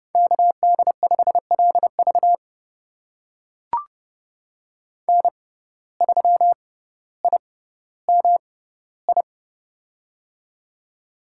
Morse code, heard as a keyed tone, about 30 wpm, spaced as for 15 wpm, 700 Hz. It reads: KD5LV N3IMI
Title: USA Call Signs - 5 Characters - 6x Character Spacing - 30wpm